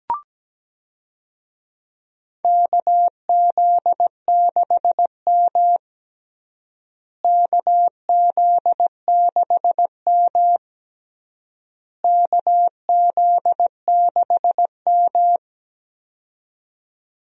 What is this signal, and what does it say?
Morse code, heard as a keyed tone, about 17 wpm, 700 Hz. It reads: KZ6M KZ6M KZ6M
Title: USA Call Signs - 4 Characters - Repeated 3x - 17wpm